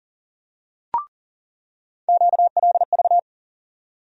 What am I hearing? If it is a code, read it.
Morse code, heard as a keyed tone, about 40 wpm, 700 Hz. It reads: QLV
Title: USA Call Sign Postfixes - 3 Letters 40wpm